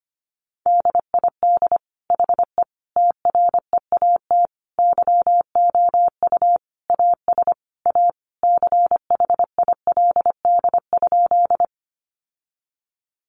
Morse code, heard as a keyed tone, about 25 wpm, 700 Hz. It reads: DID HE TREAT YOU AS A CHILD?
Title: Sentences from Top 1300 Words 25wpm